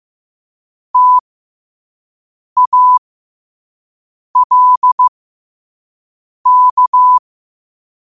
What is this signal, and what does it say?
Morse code, heard as a keyed tone, about 15 wpm, spaced as for 5 wpm, 1000 Hz. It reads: TALK
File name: qotdfiles / 5wpm